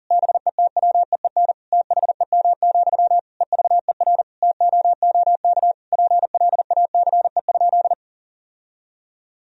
Morse code, heard as a keyed tone, about 40 wpm, 700 Hz. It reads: BETWEEN THEM, EVER TOOK PLACE?
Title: Sentences from Top 1000 Words - Review Part 1 - 40wpm